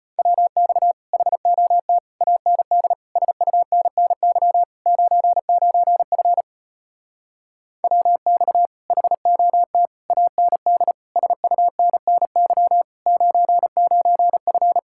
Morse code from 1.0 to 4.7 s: HOT AND SUNNY

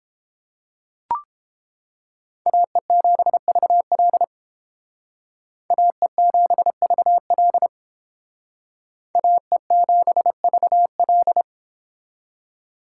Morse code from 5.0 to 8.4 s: AE7VL

